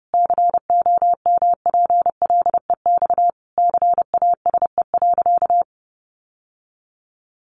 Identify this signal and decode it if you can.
Morse code, heard as a keyed tone, about 30 wpm, 700 Hz. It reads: COMPLEX CASE.
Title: Sentences from Top 900 Words 30wpm